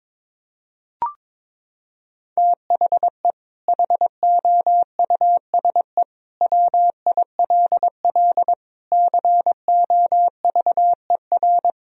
Morse code, heard as a keyed tone, about 22 wpm, 700 Hz. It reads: THE HOUSE WILL COVER
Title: Sentences from Top 1000 Words - Review Part 2 - 22wpm